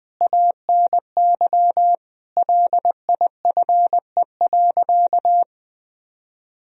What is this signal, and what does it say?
Morse code, heard as a keyed tone, about 20 wpm, 700 Hz. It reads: ANY LIFE.
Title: Sentences from Top 300 Words and Abbreviations 20wpm